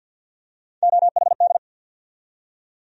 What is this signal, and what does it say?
Morse code, heard as a keyed tone, about 47 wpm, 700 Hz. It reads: OHD
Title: USA Call Sign Postfixes - 3 Letters 50wpm